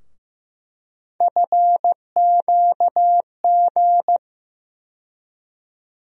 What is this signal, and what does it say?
Morse code, heard as a keyed tone, about 15 wpm, 700 Hz. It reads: FQG
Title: USA Call Sign Postfixes - 3 Letters 15wpm